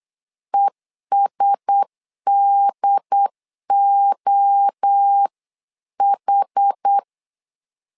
Morse code, heard as a keyed tone, about 8 wpm, 800 Hz. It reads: ESDOH
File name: LEZIONI / 4 Q1S 8wpm 2710